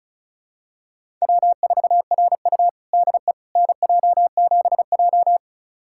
Morse code, heard as a keyed tone, about 36 wpm, 700 Hz. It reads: W4RU DE NJ7J
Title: USA Callsign DE Callsign - 4 Characters - 35wpm